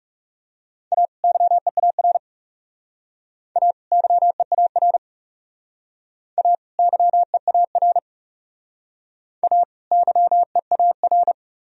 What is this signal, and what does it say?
Morse code, heard as a keyed tone, about 38 wpm, 700 Hz. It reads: A YEAR A YEAR A YEAR A YEAR
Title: Speed-Racing - Sets of 2 Words 30wpm